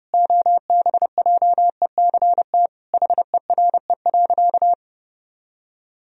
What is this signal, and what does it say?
Morse code, heard as a keyed tone, about 32 wpm, 700 Hz. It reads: OBJECT HERE.
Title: Sentences from Top 1000 Words - Review Part 3 - 30wpm